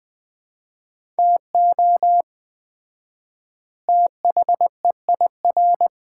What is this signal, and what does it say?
Morse code, heard as a keyed tone, about 20 wpm, 690 Hz. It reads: TO THEIR
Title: 4x Word-Spacing - Sets of 2 Words - 20wpm